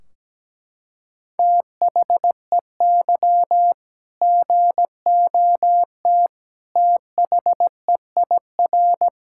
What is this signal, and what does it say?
Morse code, heard as a keyed tone, about 17 wpm, 700 Hz. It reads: THEY GOT THEIR
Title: Sentences from Top 1600 Words 17wpm